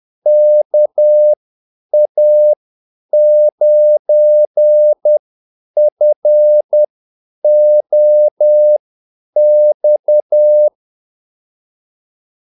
Morse code, heard as a keyed tone, about 10 wpm, spaced as for 8 wpm, 600 Hz. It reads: KA9FOX
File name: Callsigns-10-8wpm